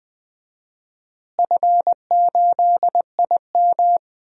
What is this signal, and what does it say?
Morse code, heard as a keyed tone, about 20 wpm, 700 Hz. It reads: F8IM
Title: DXCC-100 Call Signs - 4 Characters - Repeated 3x - 20wpm